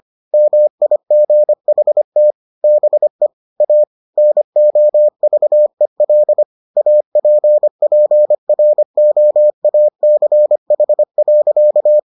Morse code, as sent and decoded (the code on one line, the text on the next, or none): -- .. --. .... - / -... / .- / -. --- ...- . .-.. / .- .--. .--. .-. --- .- -.-. .... .-.-.-
MIGHT BE A NOVEL APPROACH.